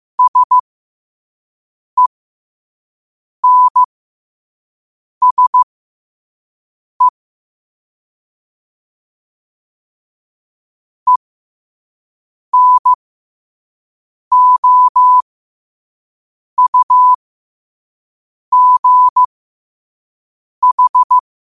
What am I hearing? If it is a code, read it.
Morse code, heard as a keyed tone, about 15 wpm, spaced as for 5 wpm, 1000 Hz. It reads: SENSE ENOUGH
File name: qotdfiles / 5wpm